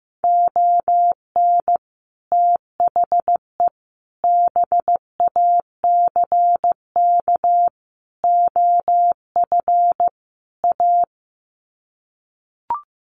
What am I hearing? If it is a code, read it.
Morse code, heard as a keyed tone, about 15 wpm, 700 Hz. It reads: ON THE BACK OF A
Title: Sets of 5 Words Encore2 15wpm